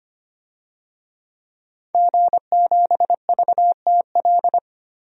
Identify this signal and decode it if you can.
Morse code, heard as a keyed tone, about 25 wpm, 700 Hz. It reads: G7VTL
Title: DXCC-100 Call Signs - 5 Characters - Repeated 3x - 25wpm